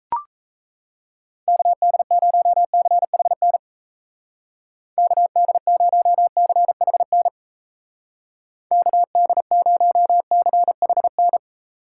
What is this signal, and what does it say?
Morse code, heard as a keyed tone, about 34 wpm, 700 Hz. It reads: KD0CHN KD0CHN KD0CHN
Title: Speed Racing - USA 2x3 Call Signs 28wpm